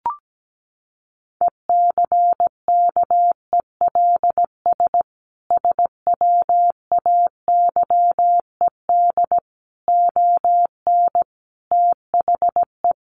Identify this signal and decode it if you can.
Morse code, heard as a keyed tone, about 17 wpm, 700 Hz. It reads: ECKELS SWAYED ON THE
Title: A Sound Of Thunder 17wpm